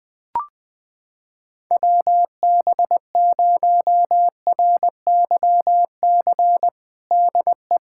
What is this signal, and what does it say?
Morse code, heard as a keyed tone, about 20 wpm, 700 Hz. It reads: WB0RYC DE
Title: USA Callsign DE Callsign 20wpm